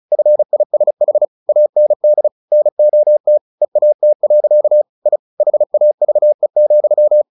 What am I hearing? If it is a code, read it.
Morse code, heard as a keyed tone, about 34 wpm, 600 Hz. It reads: FISH AND NOT EAT. I HAVE,
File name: the old man and the sea / the old man and the sea 1 35wpm